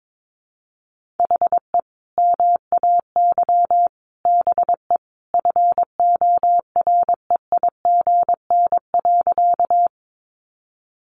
Morse code, heard as a keyed tone, about 22 wpm, 700 Hz. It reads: HE MAY BE FOREIGN.
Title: Sentences from Top 1400 Words 22wpm